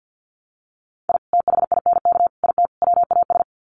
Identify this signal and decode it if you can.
Morse code, heard as a keyed tone, about 50 wpm, 700 Hz. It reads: I THINK IT WAS
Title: Sets of 4 Words 50wpm